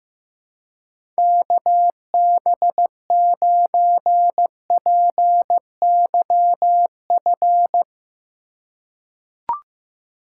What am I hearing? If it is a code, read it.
Morse code, heard as a keyed tone, about 16 wpm, 700 Hz. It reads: KB9PYF
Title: USA Call Signs - 6 Characters - 2x Character Spacing - 15wpm